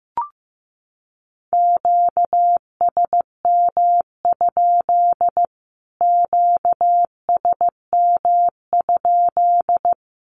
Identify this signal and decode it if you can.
Morse code, heard as a keyed tone, about 15 wpm, 700 Hz. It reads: QSM? QSM?
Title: Q-Codes Repeated 3x 15wpm